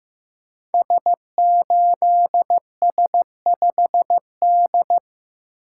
Morse code, heard as a keyed tone, about 15 wpm, 700 Hz. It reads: S8S5D
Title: Code Groups - 5 Letter:Numbers - 15wpm